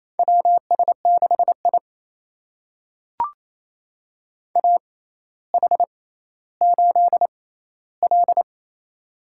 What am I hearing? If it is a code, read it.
Morse code, heard as a keyed tone, about 28 wpm, spaced as for 14 wpm, 700 Hz. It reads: WS6I AH8L
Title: USA Call Signs - 4 Characters - 6x Character Spacing - 28wpm